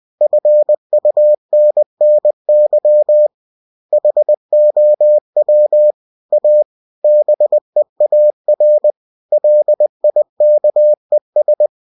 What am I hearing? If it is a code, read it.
Morse code, heard as a keyed tone, about 20 wpm, 600 Hz. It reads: FUNNY HOW A BEAR LIKES
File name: cw-books / winnie-the-pooh / winnie-20wpm-page-0003